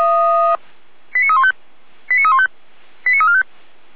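Five-tone selective call (ZVEI), heard as quick runs of five tones, tones of about 70 ms. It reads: 89315 89315 89345
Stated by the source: ZVEI five-tone selective call 89265, received via SDR.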